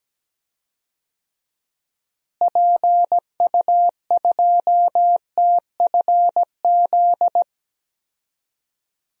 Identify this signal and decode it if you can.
Morse code, heard as a keyed tone, about 17 wpm, 700 Hz. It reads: PU2TFZ
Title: DXCC-100 Call Signs - 6 Characters - Repeated 2x - 17wpm